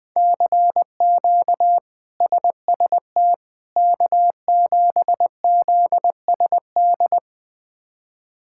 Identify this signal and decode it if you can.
Morse code, heard as a keyed tone, about 21 wpm, 700 Hz. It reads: CQ SST K7ZSD
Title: K1USN SST QSO Practice - 20wpm